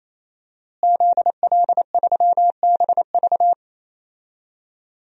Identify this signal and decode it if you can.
Morse code, heard as a keyed tone, about 28 wpm, 700 Hz. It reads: ZL3BV
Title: DXCC-100 Call Signs - 5 Characters - Repeated 3x - 28wpm